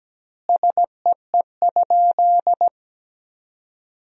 Morse code, heard as a keyed tone, about 17 wpm, 700 Hz. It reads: SEE?
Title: 2x Word-Spacing - Sentences from Top 100 Words - 17wpm